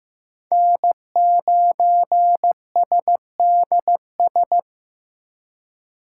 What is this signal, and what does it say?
Morse code, heard as a keyed tone, about 18 wpm, 700 Hz. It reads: N9SDS
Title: Speed Racing - USA 1x3 Call Signs 15wpm